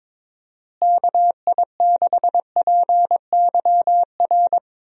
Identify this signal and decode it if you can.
Morse code, heard as a keyed tone, about 22 wpm, 700 Hz. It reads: KI6PYR
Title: USA Call Signs - 6 Characters - Repeated 2x - 22wpm